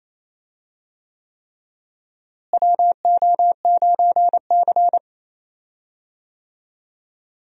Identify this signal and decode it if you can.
Morse code, heard as a keyed tone, about 28 wpm, 700 Hz. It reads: WO9C
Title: USA Call Signs - 4 Characters - Repeated 3x - 28wpm